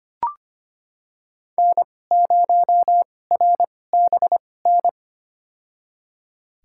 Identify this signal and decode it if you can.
Morse code, heard as a keyed tone, about 25 wpm, spaced as for 18 wpm, 700 Hz. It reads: N0RBN
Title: USA Call Signs - 5 Characters - 2x Character Spacing - 25wpm